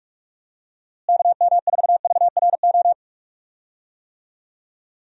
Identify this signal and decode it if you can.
Morse code, heard as a keyed tone, about 45 wpm, 700 Hz. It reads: KM4VRO
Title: USA Call Signs - 6 Characters - Repeated 3x - 45wpm